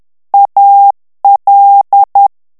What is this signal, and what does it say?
Morse code, heard as a keyed tone, about 10 wpm, 800 Hz. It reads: AL
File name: ITALIEN texte clair 2 vitesse 10 mots minute